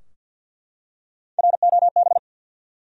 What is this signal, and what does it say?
Morse code, heard as a keyed tone, about 52 wpm, 700 Hz. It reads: ROB
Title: USA Common Names - Max 5 Letters - 50wpm